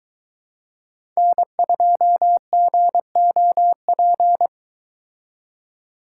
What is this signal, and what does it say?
Morse code, heard as a keyed tone, about 22 wpm, 700 Hz. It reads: N2GOP